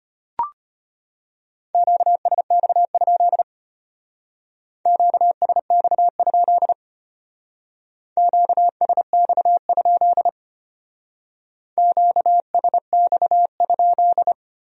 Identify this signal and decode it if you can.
Morse code, heard as a keyed tone, about 30 wpm, 700 Hz. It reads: QSX? QSX? QSX? QSX?
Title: Speed-Racing Q-Codes - 25wpm